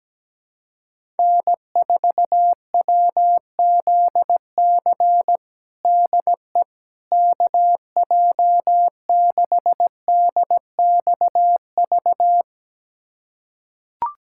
Morse code, heard as a keyed tone, about 17 wpm, 700 Hz. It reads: N4WZC DE KJ6DXV
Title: USA Callsign DE Callsign 17wpm